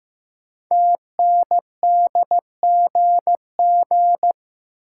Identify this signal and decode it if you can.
Morse code, heard as a keyed tone, about 15 wpm, 700 Hz. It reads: TNDGG
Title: Code Groups - 5 Letters - 15wpm